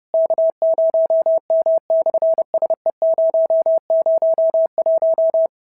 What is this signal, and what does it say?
Morse code, heard as a keyed tone, about 30 wpm, 650 Hz. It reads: K0M/SE001
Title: SOTA QSO Practice 30wpm